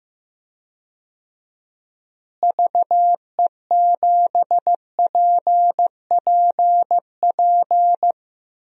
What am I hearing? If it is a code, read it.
Morse code, heard as a keyed tone, about 15 wpm, 700 Hz. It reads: VE7PPP
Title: DXCC-100 Call Signs - 6 Characters - Repeated 3x - 15wpm